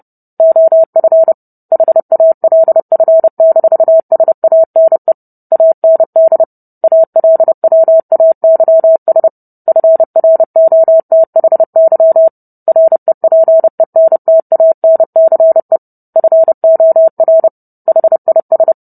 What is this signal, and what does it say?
Morse code, heard as a keyed tone, about 30 wpm, 650 Hz. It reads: OF HALF-SANE AND ALWAYS FROTHY REPENTANCE FOR HIS